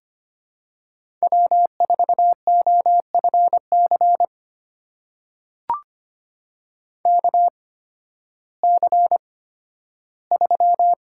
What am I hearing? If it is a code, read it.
Morse code, heard as a keyed tone, about 25 wpm, spaced as for 13 wpm, 700 Hz. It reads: W4OFC KC3
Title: USA Call Signs - 8x Character Spacing - 25wpm